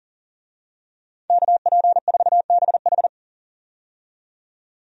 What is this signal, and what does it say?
Morse code, heard as a keyed tone, about 40 wpm, 700 Hz. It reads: KP4BH
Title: DXCC-100 Call Signs - 5 Characters - 40wpm